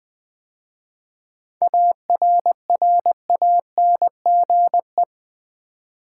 Morse code, heard as a keyed tone, about 20 wpm, 700 Hz. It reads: ARRANGE
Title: 7 Letter Words - First 3 Letters Given - 20wpm